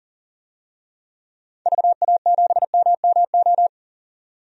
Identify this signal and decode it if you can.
Morse code, heard as a keyed tone, about 39 wpm, 700 Hz. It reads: VA7MMO
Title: DXCC-100 Call Signs - 6 Characters - 40wpm